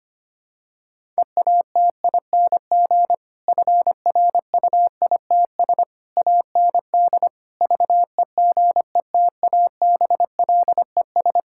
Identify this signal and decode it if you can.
Morse code, heard as a keyed tone, about 25 wpm, 700 Hz. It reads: EATING FRUITS AND VEGETABLES
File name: practice / 25wpm / Difficult Sentences with Hints - 25wpm